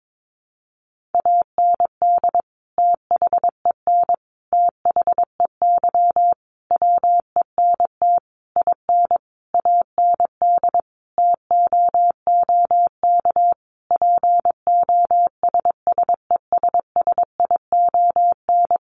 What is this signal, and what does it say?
Morse code, heard as a keyed tone, about 22 wpm, 700 Hz. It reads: AND THEN THEY WENT IN AND TOOK POSSESSION
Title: Sentences from Top 1400 Words 22wpm